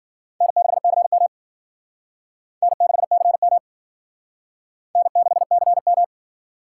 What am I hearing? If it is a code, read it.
Morse code, heard as a keyed tone, about 50 wpm, 700 Hz. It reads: N6CM N6CM N6CM
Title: Speed Racing - USA 1x2 Call Signs 40wpm